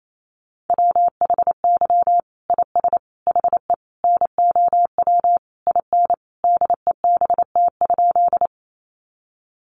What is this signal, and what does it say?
Morse code, heard as a keyed tone, about 28 wpm, 700 Hz. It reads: WHY IS HE NOW IN DEBT?
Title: Sentences from Top 1300 Words 28wpm